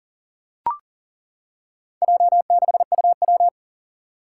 Morse code, heard as a keyed tone, about 40 wpm, 700 Hz. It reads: J6UW
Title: Four Challenging Letter-Number Combinations 40wpm